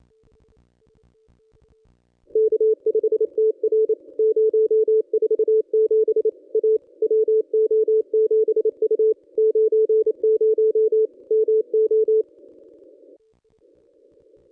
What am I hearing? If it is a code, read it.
Morse code, heard as a keyed tone, about 28 wpm, 440 Hz. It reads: K5TR 047 A WO7U 90 MO
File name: contests / ss-cw-k5tr-op-n5rz -1200